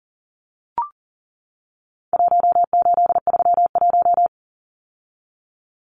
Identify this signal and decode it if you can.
Morse code, heard as a keyed tone, about 37 wpm, 700 Hz. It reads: 1831
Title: Numbers - Contest Style - 4-Digit 40wpm